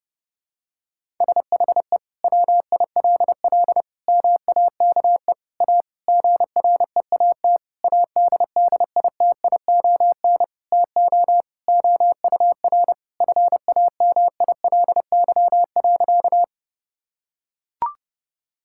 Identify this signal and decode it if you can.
Morse code, heard as a keyed tone, about 31 wpm, 700 Hz. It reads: SHE WILL MAKE A GREAT ADDITION TO OUR FAMILY.
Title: Sentences from Top 1000 Words - Review Part 6 - 30wpm